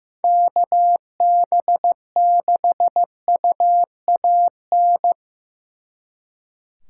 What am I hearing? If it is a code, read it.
Morse code, heard as a keyed tone, about 18 wpm, 700 Hz. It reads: KB6UAN
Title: Speed Racing - USA 2x3 Call Signs 15wpm